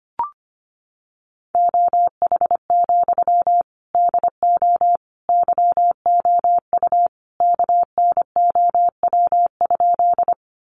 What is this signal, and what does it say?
Morse code, heard as a keyed tone, about 25 wpm, 700 Hz. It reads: OH, DO YOU KNOW?
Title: Sentences from Top 500 Words 25wpm